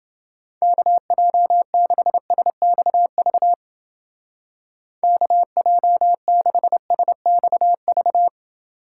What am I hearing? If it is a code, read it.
Morse code, heard as a keyed tone, about 29 wpm, 700 Hz. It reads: KJ6SXV KJ6SXV